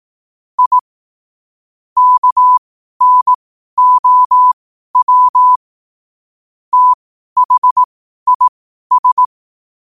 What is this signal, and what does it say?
Morse code, heard as a keyed tone, about 18 wpm, spaced as for 12 wpm, 1000 Hz. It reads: I KNOW THIS
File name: qotdfiles / 12wpm